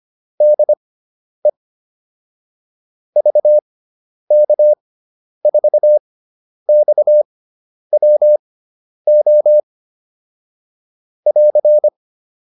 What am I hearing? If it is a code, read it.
Morse code, heard as a keyed tone, about 25 wpm, spaced as for 10 wpm, 600 Hz. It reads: DE VK4XWO <AR>
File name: QSO407 10